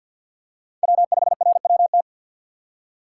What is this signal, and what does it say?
Morse code, heard as a keyed tone, about 53 wpm, 700 Hz. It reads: W5RWT
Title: USA Call Signs - 5 Characters - 2x Character Spacing - 50wpm